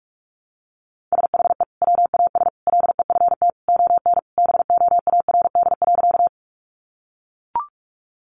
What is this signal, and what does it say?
Morse code, heard as a keyed tone, about 44 wpm, 700 Hz. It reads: SHE WAS LEFT ON BOARD.
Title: Sentences from Top 700 Words 45wpm